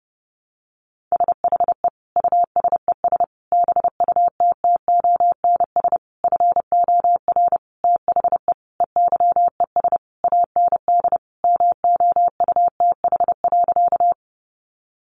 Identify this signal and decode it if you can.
Morse code, heard as a keyed tone, about 30 wpm, 700 Hz. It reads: SHE USES BUTTONS FOR THE EYES AND MOUTH.
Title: Easy Phrases 30wpm